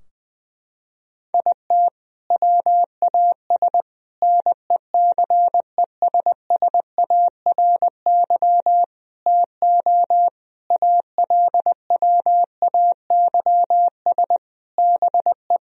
Morse code, heard as a keyed tone, about 21 wpm, 700 Hz. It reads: IT WAS NECESSARY TO ALWAYS BE
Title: Sentences from Top 1000 Words - Review Part 4 - 20wpm